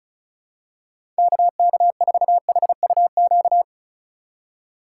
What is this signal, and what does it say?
Morse code, heard as a keyed tone, about 34 wpm, 700 Hz. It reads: KK4HUQ
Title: USA Call Signs - 6 Characters - 2x Character Spacing - 35wpm